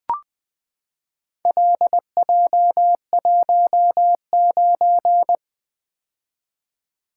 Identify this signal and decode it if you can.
Morse code, heard as a keyed tone, about 20 wpm, 700 Hz. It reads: LJ19
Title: Four Letter-Number Combinations - 20wpm